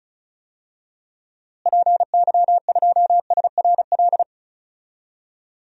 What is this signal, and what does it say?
Morse code, heard as a keyed tone, about 36 wpm, 700 Hz. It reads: PY2SRL
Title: DXCC-100 Call Signs - 6 Characters - Repeated 2x - 35wpm